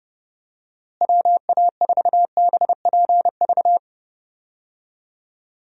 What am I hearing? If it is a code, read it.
Morse code, heard as a keyed tone, about 30 wpm, 700 Hz. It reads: WA4BPV
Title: USA Call Signs - 6 Characters - Repeated 3x - 30wpm